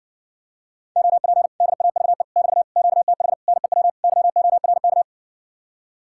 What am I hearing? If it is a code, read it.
Morse code, heard as a keyed tone, about 60 wpm, 700 Hz. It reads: OP DAVE <BT> QTH NEW YORK